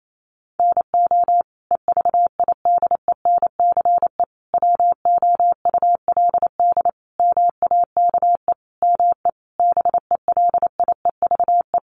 Morse code, heard as a keyed tone, about 28 wpm, 700 Hz. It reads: NO EVIDENCE WOULD MAKE ME BELIEVE
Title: Sentences from Top 1300 Words 28wpm